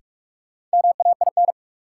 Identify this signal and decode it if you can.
Morse code, heard as a keyed tone, about 42 wpm, 700 Hz. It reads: MAIN
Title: Sentences from Top 1000 Words - Review Part 2 - New Word Drill - 45wpm